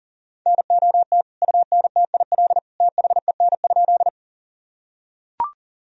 Morse code, heard as a keyed tone, about 37 wpm, 700 Hz. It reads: NOT UNTIL THEN?
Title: Sentences from Top 1000 Words - Review Part 1 - 40wpm